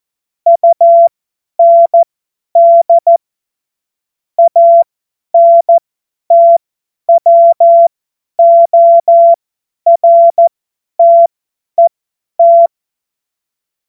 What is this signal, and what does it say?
Morse code, heard as a keyed tone, about 14 wpm, spaced as for 10 wpm, 680 Hz. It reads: UND ANTWORTET
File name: Morsekurs / Witze / Witz 11. Januar 2022, 50 F70